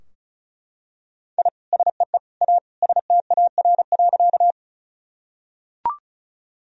Morse code, heard as a keyed tone, about 34 wpm, 700 Hz. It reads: I SEE A STAR.
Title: Sentences from Top 1000 Words - Review Part 2 - 35wpm